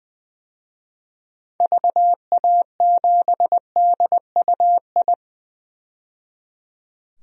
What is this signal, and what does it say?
Morse code, heard as a keyed tone, about 20 wpm, 700 Hz. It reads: VA7DUI
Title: DXCC-100 Call Signs - 6 Characters - 20wpm